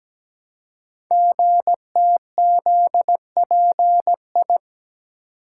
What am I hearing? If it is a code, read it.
Morse code, heard as a keyed tone, about 17 wpm, 700 Hz. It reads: GTZPI
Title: Code Groups - 5 Letters - 17wpm